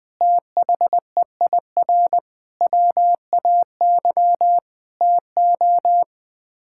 Morse code, heard as a keyed tone, about 20 wpm, 700 Hz. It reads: THEIR WAY TO
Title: Sets of 6 Words - 20wpm